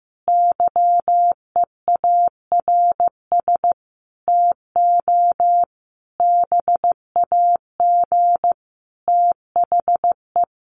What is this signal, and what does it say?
Morse code, heard as a keyed tone, about 15 wpm, 700 Hz. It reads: YEARS TO BAG THE
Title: A Sound Of Thunder 15wpm